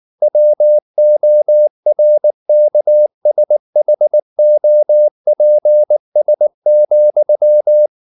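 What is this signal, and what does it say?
Morse code, heard as a keyed tone, about 19 wpm, 600 Hz. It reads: WORKSHOPS,
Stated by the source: Morse code at 19 wpm from CW med Gnister, reading Ken Follett's novel Never in English.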